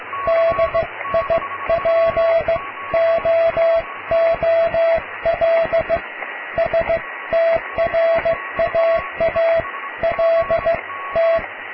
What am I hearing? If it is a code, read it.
Morse code, heard as a keyed tone, about 15 wpm, spaced as for 13 wpm, 660 Hz, with radio noise. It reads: DIPOOL STRAALT
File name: cw / tekst(a)+QRM